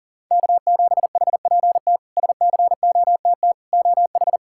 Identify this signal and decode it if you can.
Morse code, heard as a keyed tone, about 40 wpm, 700 Hz. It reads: K7HPT SCOTT OH
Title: K1USN SST QSO Practice - 40wpm